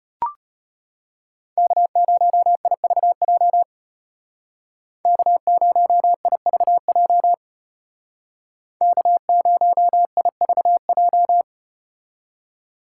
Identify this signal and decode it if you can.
Morse code, heard as a keyed tone, about 34 wpm, 700 Hz. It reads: K0IVJ K0IVJ K0IVJ